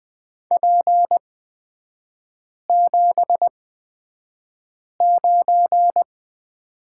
Morse code, heard as a keyed tone, about 20 wpm, spaced as for 5 wpm, 690 Hz. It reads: P79
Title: 229 Lesson 37e - Letter x - Three Characters 20wpm